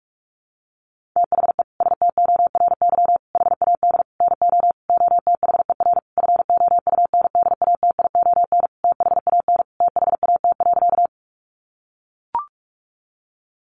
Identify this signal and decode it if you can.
Morse code, heard as a keyed tone, about 44 wpm, 700 Hz. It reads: THE STORY HAD NO OTHER FOUNDATION THAN THAT.
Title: Sentences from Top 1200 Words 45wpm